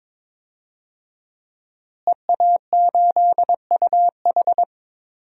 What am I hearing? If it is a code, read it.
Morse code, heard as a keyed tone, about 22 wpm, 700 Hz. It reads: EA8UH
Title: DXCC-100 Call Signs - 5 Characters - Repeated 2x - 22wpm